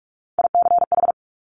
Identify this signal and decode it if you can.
Morse code, heard as a keyed tone, about 46 wpm, 700 Hz. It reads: ICH
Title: Top Three Letter Combinations in Words 45wpm